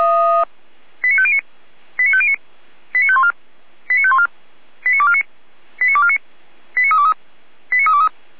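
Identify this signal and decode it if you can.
Five-tone selective call (ZVEI), heard as quick runs of five tones, tones of about 70 ms. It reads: 89509 89509 89514 89514 89249 89249 89323 89323
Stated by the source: ZVEI five-tone call 89337 received on an SDR, from feuerwehr-orlamuende.org (fire brigade Orlamünde).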